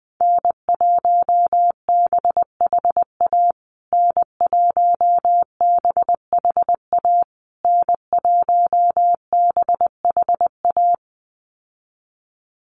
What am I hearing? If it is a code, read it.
Morse code, heard as a keyed tone, about 19 wpm, 700 Hz. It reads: N1BHA N1BHA N1BHA
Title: USA Call Signs Repeated 3x 20wpm